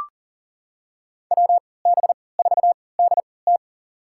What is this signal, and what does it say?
Morse code, heard as a keyed tone, about 40 wpm, spaced as for 23 wpm, 700 Hz. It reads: WB4DT
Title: USA Call Signs - 3x Character Spacing - 40wpm